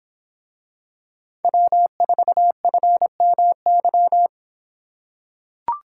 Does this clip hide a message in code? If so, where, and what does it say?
Yes: Morse code from 1.4 to 4.3 s, heard as a keyed tone, about 24 wpm, 700 Hz. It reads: W4FMY